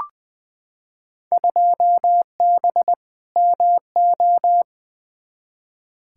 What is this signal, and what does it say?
Morse code, heard as a keyed tone, about 21 wpm, 700 Hz. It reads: 2B MO
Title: Field Day Exchanges Encore - 20wpm